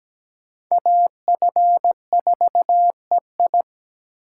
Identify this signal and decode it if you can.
Morse code, heard as a keyed tone, about 17 wpm, 700 Hz. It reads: AF4EI
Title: USA Call Signs - 5 Characters - 2x Character Spacing - 17wpm